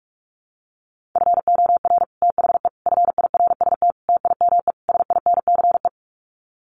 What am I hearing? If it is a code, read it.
Morse code, heard as a keyed tone, about 44 wpm, 700 Hz. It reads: FOR THE FIRST TIME SINCE
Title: Sets of 5 Words 45wpm